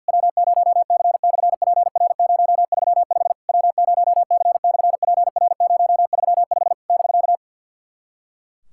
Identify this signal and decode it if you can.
Morse code, heard as a keyed tone, about 51 wpm, 700 Hz. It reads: W0C/PR035 W0C/PR035 <BK>
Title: SOTA QSO Practice 50wpm